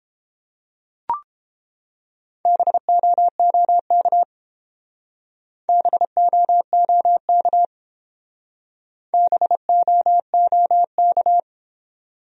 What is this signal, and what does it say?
Morse code, heard as a keyed tone, about 28 wpm, 700 Hz. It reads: BOOK BOOK BOOK